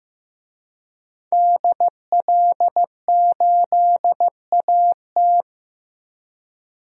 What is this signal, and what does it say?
Morse code, heard as a keyed tone, about 15 wpm, 700 Hz. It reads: DL8AT
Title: DXCC-100 Call Signs - 5 Characters - Repeated 3x - 15wpm